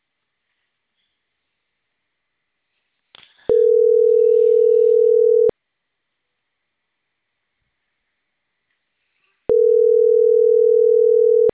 Ringback tone (North America), 440 + 480 Hz.